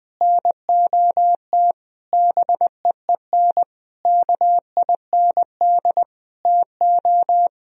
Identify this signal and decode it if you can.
Morse code, heard as a keyed tone, about 20 wpm, 700 Hz. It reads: NOT BEEN KIND TO
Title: Sentences from Top 1000 Words - Review Part 2 - 20wpm